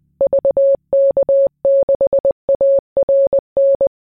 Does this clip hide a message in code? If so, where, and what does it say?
Yes: Morse code from 0.2 to 3.9 s, heard as a keyed tone, about 20 wpm, 550 Hz. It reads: VK6ARN